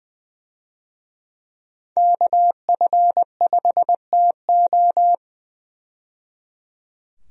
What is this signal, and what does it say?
Morse code, heard as a keyed tone, about 20 wpm, 700 Hz. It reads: KF5TO
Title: USA Call Signs - 5 Characters - 20wpm